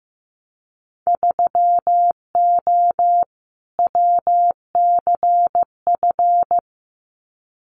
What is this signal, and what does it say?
Morse code, heard as a keyed tone, about 15 wpm, 700 Hz. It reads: 3O WCF
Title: Winter Field Day Exchanges Encore 15wpm